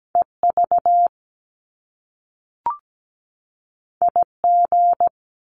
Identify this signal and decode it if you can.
Morse code, heard as a keyed tone, about 17 wpm, 700 Hz. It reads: EV IG
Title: Top Two Letter Combinations in Words 17wpm